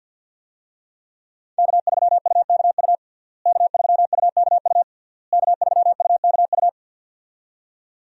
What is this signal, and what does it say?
Morse code, heard as a keyed tone, about 51 wpm, 700 Hz. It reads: K3UKU K3UKU K3UKU
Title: USA Call Signs - 5 Characters - Repeated 3x - 50wpm